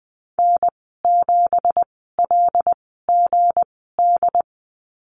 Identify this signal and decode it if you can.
Morse code, heard as a keyed tone, about 20 wpm, spaced as for 15 wpm, 700 Hz. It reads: N7LGD
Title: 179 Lesson 29g - Slash - Call Signs 20wpm